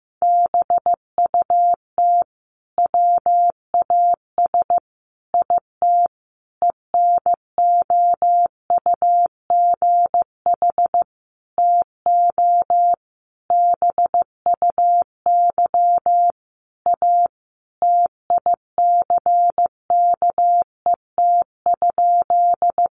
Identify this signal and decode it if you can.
Morse code, heard as a keyed tone, about 14 wpm, 700 Hz. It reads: BUT WAS IT ENOUGH TO BUY A TICKET?
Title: Sentences from Top 1400 Words 15wpm